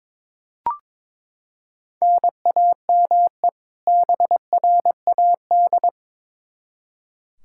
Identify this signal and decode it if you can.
Morse code, heard as a keyed tone, about 22 wpm, 700 Hz. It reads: NAME BRAD
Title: Practice QSO - Exchange 1 - Beginning - 22wpm